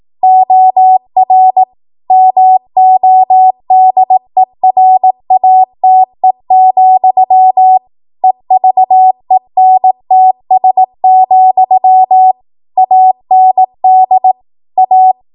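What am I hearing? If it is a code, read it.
Morse code, heard as a keyed tone, about 18 wpm, 750 Hz. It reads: OR MODERATE, EVENTS, AND A
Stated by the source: Code practice at 18 WPM, a transition run bridging 15 to 20 WPM.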